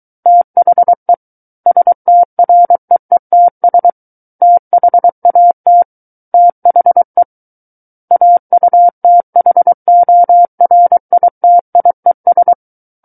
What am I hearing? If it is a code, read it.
Morse code, heard as a keyed tone, about 23 wpm, 700 Hz. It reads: THE STREETS THAT THE AUTHORITIES
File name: Book One / 14-Chapter 14 - 23 WPM